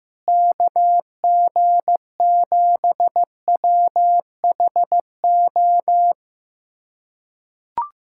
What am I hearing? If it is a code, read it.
Morse code, heard as a keyed tone, about 15 wpm, 700 Hz. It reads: KG7WHO
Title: USA Callsign DE Callsign 15wpm